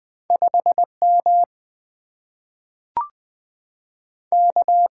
Morse code, heard as a keyed tone, about 20 wpm, 700 Hz. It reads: 5M K
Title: Two Letter-Number Combinations - 20wpm